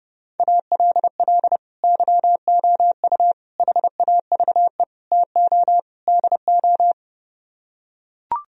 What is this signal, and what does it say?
Morse code, heard as a keyed tone, about 31 wpm, 700 Hz. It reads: ALL YOU HAVE TO DO